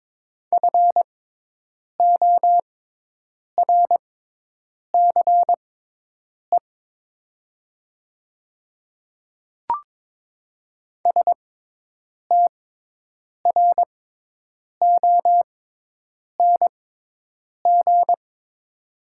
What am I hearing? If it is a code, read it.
Morse code, heard as a keyed tone, about 22 wpm, spaced as for 8 wpm, 700 Hz. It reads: FORCE STRONG
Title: 6x Character Spacing - Top 500 Words - 22wpm